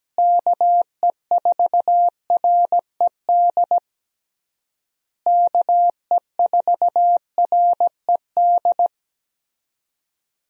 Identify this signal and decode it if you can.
Morse code, heard as a keyed tone, about 17 wpm, 700 Hz. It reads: KE4RED KE4RED
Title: USA Call Signs - 6 Characters - Repeated 3x - 17wpm